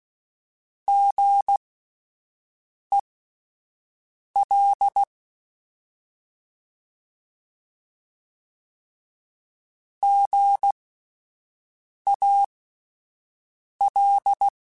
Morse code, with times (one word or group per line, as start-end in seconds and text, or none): 0.9-5.0 GEL
10.0-14.5 GAL